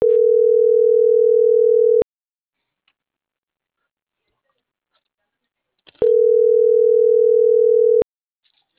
Ringback tone (North America), 440 + 480 Hz.